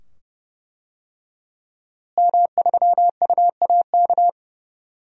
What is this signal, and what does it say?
Morse code, heard as a keyed tone, about 29 wpm, 700 Hz. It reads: M3UAK
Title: DXCC-100 Call Signs - 5 Characters - Repeated 2x - 30wpm